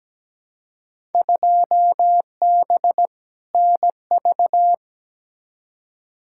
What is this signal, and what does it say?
Morse code, heard as a keyed tone, about 17 wpm, 700 Hz. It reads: 2B NV
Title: Field Day Exchanges Encore - 17wpm